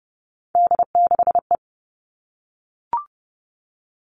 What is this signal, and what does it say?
Morse code, heard as a keyed tone, about 30 wpm, 700 Hz. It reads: D6E
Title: All Cut Numbers - 3-Digit 30wpm